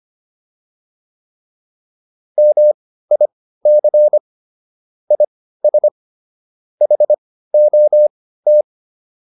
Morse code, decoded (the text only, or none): MIC IS HOT